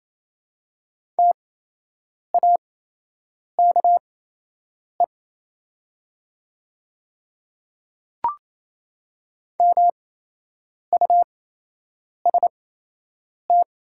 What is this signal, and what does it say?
Morse code, heard as a keyed tone, about 28 wpm, spaced as for 8 wpm, 700 Hz. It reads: TAKE MUST
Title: 8x Character Spacing - Top 200 Words - 28wpm